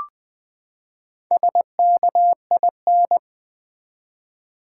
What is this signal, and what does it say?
Morse code, heard as a keyed tone, about 20 wpm, 700 Hz. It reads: SKIN
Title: Four Letter Words - 20wpm